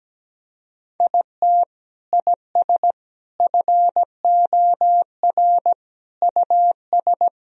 Morse code, heard as a keyed tone, about 17 wpm, 700 Hz. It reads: IT IS FOR US
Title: Sentences from Top 1000 Words - Review Part 4 - 17wpm